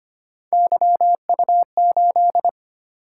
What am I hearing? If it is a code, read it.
Morse code, heard as a keyed tone, about 25 wpm, 700 Hz. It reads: YU8
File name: Three Letter-Number Combinations - 25wpm